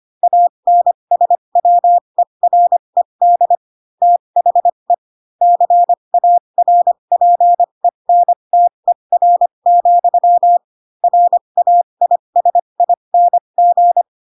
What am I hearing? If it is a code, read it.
Morse code, heard as a keyed tone, about 25 wpm, spaced as for 22 wpm, 700 Hz. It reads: ANSWERED THE CARPENTER, RAISING
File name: Pinocchio / Pinocchio ch2 18 25 ik1hge